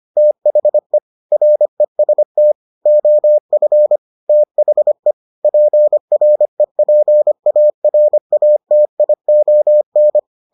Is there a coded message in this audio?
Yes: Morse code from 0.2 to 10.2 s, heard as a keyed tone, about 25 wpm, 600 Hz. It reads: THE REST OF THE PREPARATION